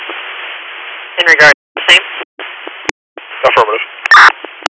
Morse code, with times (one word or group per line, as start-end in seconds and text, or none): none